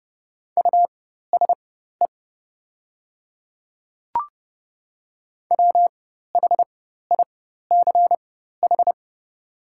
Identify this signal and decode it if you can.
Morse code, heard as a keyed tone, about 30 wpm, spaced as for 14 wpm, 700 Hz. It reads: USE WHICH
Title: 4x Character Spacing - Top 100 Words - 30wpm